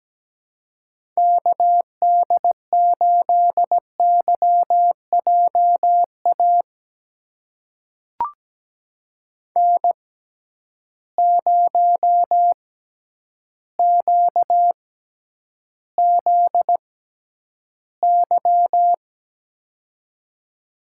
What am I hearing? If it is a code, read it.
Morse code, heard as a keyed tone, about 17 wpm, spaced as for 9 wpm, 700 Hz. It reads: KD8YJA N0QZY
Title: USA Call Signs - 6x Character Spacing - 17wpm